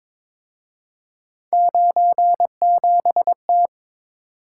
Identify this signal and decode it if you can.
Morse code, heard as a keyed tone, about 26 wpm, 700 Hz. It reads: 97T
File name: Speed-Racing - Three Letter-Number Combinations - 22wpm